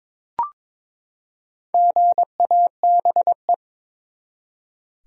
Morse code, heard as a keyed tone, about 22 wpm, 700 Hz. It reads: GABE
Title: USA Common Names - Max 4 Letters - 22wpm